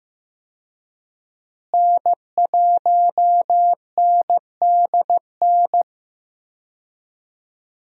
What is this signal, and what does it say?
Morse code, heard as a keyed tone, about 15 wpm, 700 Hz. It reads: N1NDN
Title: USA Call Signs - 5 Characters - Repeated 2x - 15wpm